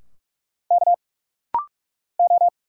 Morse code, heard as a keyed tone, about 46 wpm, 690 Hz. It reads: K O
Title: Single Letters - Rapid-Fire 45wpm